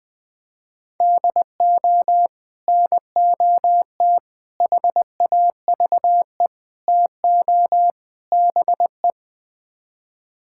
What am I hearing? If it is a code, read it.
Morse code, heard as a keyed tone, about 20 wpm, 700 Hz. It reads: DO NOT HAVE TO BE
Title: Build A Phrase - 5-Word Phrases - 20wpm